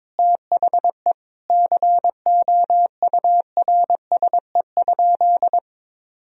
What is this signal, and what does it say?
Morse code, heard as a keyed tone, about 22 wpm, 700 Hz. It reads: THE COURSE?
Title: Sentences from Top 1000 Words - Review Part 2 - 22wpm